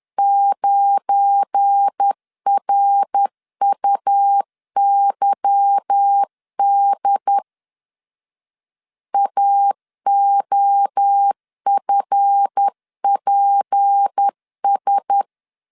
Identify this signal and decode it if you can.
Morse code, heard as a keyed tone, about 10 wpm, 800 Hz. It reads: 9RUYD AOFPS
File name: LEZIONI / 8 YJ45 / 8 Y 10wpm 2615